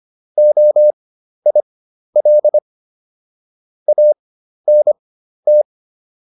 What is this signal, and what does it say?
Morse code, heard as a keyed tone, about 25 wpm, spaced as for 12 wpm, 600 Hz. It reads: OIL ANT